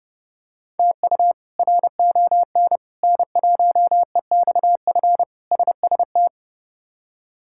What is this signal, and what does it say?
Morse code, heard as a keyed tone, about 29 wpm, 700 Hz. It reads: TU RON N1EXF SST
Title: K1USN SST QSO Practice - 30wpm